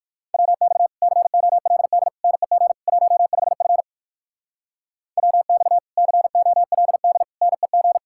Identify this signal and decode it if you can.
Morse code, heard as a keyed tone, about 58 wpm, 700 Hz. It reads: WX COLD NEG 15F WX COLD NEG